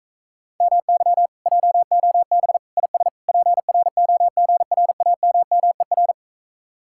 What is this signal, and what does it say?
Morse code, heard as a keyed tone, about 46 wpm, 700 Hz. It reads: MY JOB IS PROGRAMMER